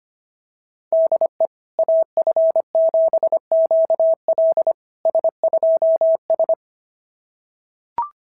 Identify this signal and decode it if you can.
Morse code, heard as a keyed tone, about 25 wpm, 650 Hz. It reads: DE AF7QL S2S
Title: SOTA QSO Practice 25wpm